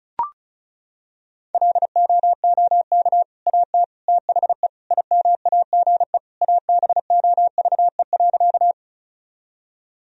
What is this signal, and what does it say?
Morse code, heard as a keyed tone, about 36 wpm, 700 Hz. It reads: LOOK AT THE IMAGE ABOVE.
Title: Sentences from Top 1000 Words - Review Part 4 - 35wpm